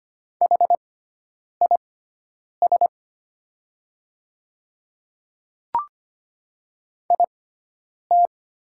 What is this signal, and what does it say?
Morse code, heard as a keyed tone, about 25 wpm, spaced as for 9 wpm, 700 Hz. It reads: HIS IT
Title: 6x Character Spacing - Top 200 Words - 25wpm